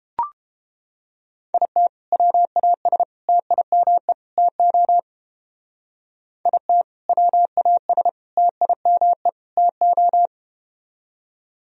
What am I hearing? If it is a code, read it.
Morse code, heard as a keyed tone, about 30 wpm, 700 Hz. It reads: IT WAS TIME TO IT WAS TIME TO